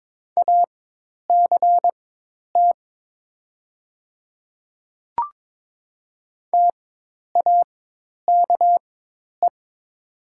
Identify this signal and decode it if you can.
Morse code, heard as a keyed tone, about 22 wpm, spaced as for 10 wpm, 700 Hz. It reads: ACT TAKE
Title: 4x Character Spacing - Top 200 Words - 22wpm